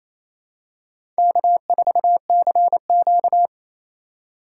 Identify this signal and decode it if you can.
Morse code, heard as a keyed tone, about 27 wpm, 700 Hz. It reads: K4CQ
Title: USA Call Signs - 4 Characters - 3x Character Spacing - 28wpm